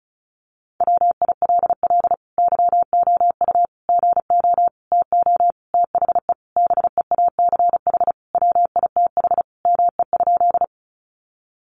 Morse code, heard as a keyed tone, about 34 wpm, 700 Hz. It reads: WILL YOU GO TO THE BEACH WITH ME?
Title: Sentences from Top 1500 Words 35wpm